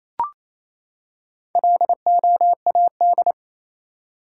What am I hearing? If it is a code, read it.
Morse code, heard as a keyed tone, about 28 wpm, 700 Hz. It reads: LOAD
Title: Four Letter Words - 28wpm